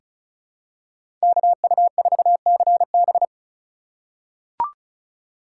Morse code, heard as a keyed tone, about 34 wpm, 700 Hz. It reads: KU4CB
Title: USA Call Signs - 5 Characters - 6x Character Spacing - 35wpm